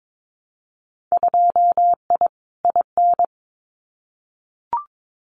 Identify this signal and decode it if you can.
Morse code, heard as a keyed tone, about 22 wpm, 700 Hz. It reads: 2I IN
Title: Winter Field Day Exchanges Encore 22wpm